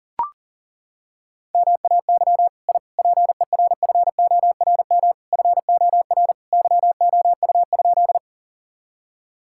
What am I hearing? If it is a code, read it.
Morse code, heard as a keyed tone, about 37 wpm, 700 Hz. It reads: MAY I PERFORM FOR YOU?
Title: Sentences from Top 1000 Words - Review Part 5 - 40wpm